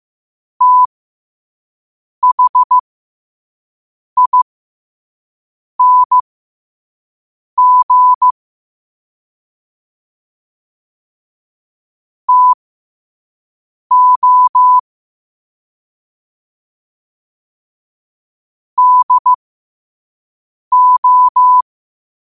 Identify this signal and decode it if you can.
Morse code, heard as a keyed tone, about 15 wpm, spaced as for 5 wpm, 1000 Hz. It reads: THING TO DO
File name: qotdfiles / 5wpm